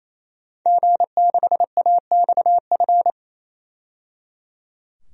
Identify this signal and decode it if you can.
Morse code, heard as a keyed tone, about 28 wpm, 700 Hz. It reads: G6AXF
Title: DXCC-100 Call Signs - 5 Characters - 28wpm